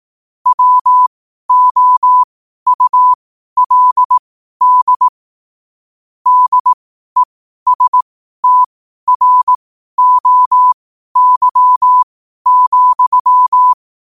Morse code, heard as a keyed tone, about 18 wpm, spaced as for 12 wpm, 1000 Hz. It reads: WOULD DESTROY,
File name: qotdfiles / 12wpm